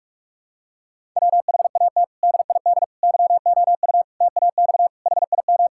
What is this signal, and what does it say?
Morse code, heard as a keyed tone, about 46 wpm, 700 Hz. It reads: WHAT DID YOU TAX HIM
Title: Sentences from Top 1000 Words - Review Part 5 - 45wpm